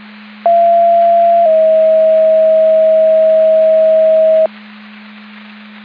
Two-tone page, heard as a one-second tone, then a three-second tone, about 680 then 640 Hz. It is heard from 0.5 to 4.5 s.